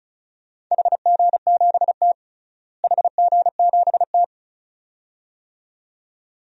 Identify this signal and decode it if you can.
Morse code, heard as a keyed tone, about 34 wpm, 700 Hz. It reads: HG7T HG7T
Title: DXCC-100 Call Signs - 4 Characters - Repeated 2x - 35wpm